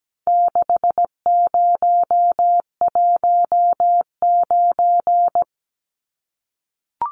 Morse code, heard as a keyed tone, about 17 wpm, 700 Hz. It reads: 6019
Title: Numbers - 4-Digit 17wpm